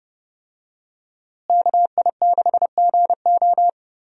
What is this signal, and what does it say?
Morse code, heard as a keyed tone, about 30 wpm, 700 Hz. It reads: KI6GO
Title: USA Call Signs - 5 Characters - Repeated 2x - 30wpm